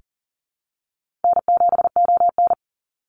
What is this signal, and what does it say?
Morse code, heard as a keyed tone, about 40 wpm, 700 Hz. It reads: N7ON
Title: Easy Call Signs - 40wpm